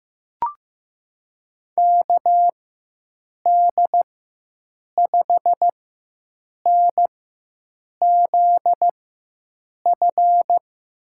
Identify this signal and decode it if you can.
Morse code, heard as a keyed tone, about 15 wpm, spaced as for 7 wpm, 700 Hz. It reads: KD5NZF
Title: USA Call Signs - 6 Characters - 4x Character Spacing - 15wpm